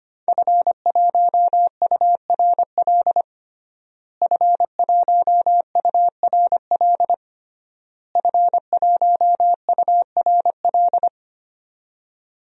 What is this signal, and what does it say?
Morse code, heard as a keyed tone, about 25 wpm, 700 Hz. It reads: F1URL F1URL F1URL
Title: DXCC-100 Call Signs - 5 Characters - Repeated 3x - 25wpm